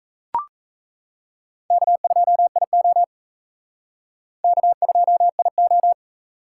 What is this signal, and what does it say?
Morse code, heard as a keyed tone, about 36 wpm, 700 Hz. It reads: K2IO K2IO